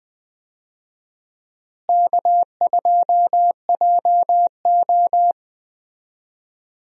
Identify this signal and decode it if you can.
Morse code, heard as a keyed tone, about 20 wpm, 700 Hz. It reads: K2JO
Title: USA Call Signs - 4 Characters - Repeated 2x - 20wpm